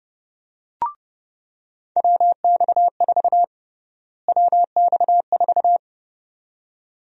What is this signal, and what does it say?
Morse code, heard as a keyed tone, about 30 wpm, 700 Hz. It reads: WX4 WX4
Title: USA Call Sign Prefixes - Repeated 2x - 30wpm